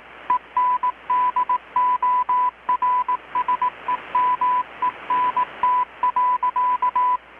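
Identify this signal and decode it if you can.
Morse code, heard as a keyed tone, about 18 wpm, 1000 Hz. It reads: ENDORSEMENT.
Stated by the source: Morse from KPH's Night of Nights XX maritime radiotelegraphy event, received in Chiapas, Mexico about 3,800 km away on a Kenwood R-600 with a grounded 10 m long-wire antenna.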